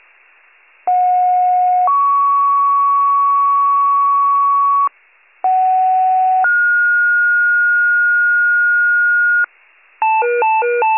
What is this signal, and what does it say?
Two-tone page, heard as a one-second tone, then a three-second tone, about 730 then 1120 Hz, about 750 then 1430 Hz, with noise. It is heard from 0.9 to 9.4 s.